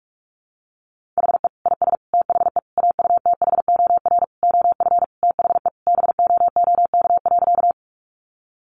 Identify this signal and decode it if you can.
Morse code, heard as a keyed tone, about 44 wpm, 700 Hz. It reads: HE IS THE AUTHOR OF THE BOOK.
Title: Sentences from Top 1200 Words 45wpm